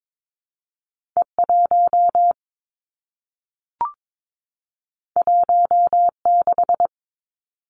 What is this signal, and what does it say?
Morse code, heard as a keyed tone, about 22 wpm, 700 Hz. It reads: E 1 1 6
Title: Two Challenging Letter-Number Combinations 22wpm